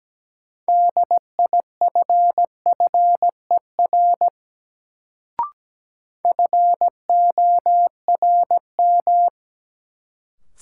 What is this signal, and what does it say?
Morse code, heard as a keyed tone, about 17 wpm, 700 Hz. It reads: DIFFER FORM
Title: Sentences from Top 1100 Words - New Word Drill Alt - 17wpm